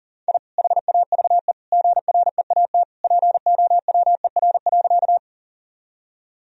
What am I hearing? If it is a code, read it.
Morse code, heard as a keyed tone, about 40 wpm, 700 Hz. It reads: I HAVE GREAT POWER.
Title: Sentences from Top 1000 Words - Review Part 2 - 40wpm